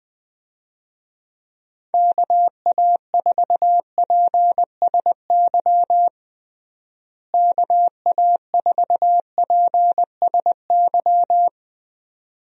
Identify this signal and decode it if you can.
Morse code, heard as a keyed tone, about 20 wpm, 700 Hz. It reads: KA4PSY KA4PSY
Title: USA Call Signs - 6 Characters - Repeated 2x - 20wpm